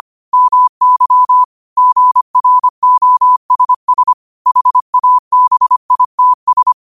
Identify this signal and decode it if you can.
Morse code, heard as a keyed tone, about 25 wpm, 1000 Hz. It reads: MY GROSS HABITS
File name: qotdfiles / 25wpm